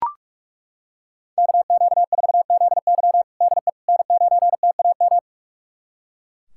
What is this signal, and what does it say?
Morse code, heard as a keyed tone, about 41 wpm, 700 Hz. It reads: KQ4ZY DE N9TAM
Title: USA Callsign DE Callsign - 5 Characters - 45wpm